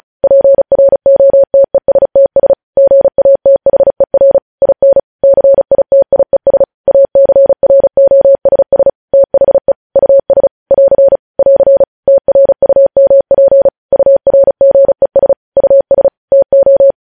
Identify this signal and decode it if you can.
Morse code, heard as a keyed tone, about 36 wpm, 550 Hz. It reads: PROTESTS GATHER IN CITIES ACROSS THE US <AR> <AR> TRUMP URGES US TO
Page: News Headlines in Morse Code at 35 WPM